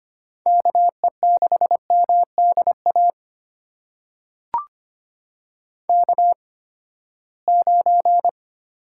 Morse code, heard as a keyed tone, about 25 wpm, spaced as for 17 wpm, 700 Hz. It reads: KE6MDA K9
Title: USA Call Signs - 8x Character Spacing - 25wpm